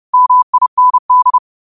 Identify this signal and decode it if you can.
Morse code, heard as a keyed tone, about 30 wpm, 1000 Hz. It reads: MIND
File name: qotdfiles / 30wpm